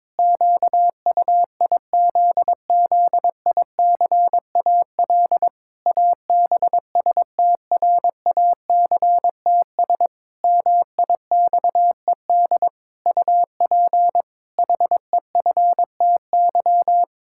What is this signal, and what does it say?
Morse code, heard as a keyed tone, about 22 wpm, 700 Hz. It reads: QUIZZICAL ABSTRACTS MIXED UP HEFTY
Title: Pangram Challenge v2 - 22wpm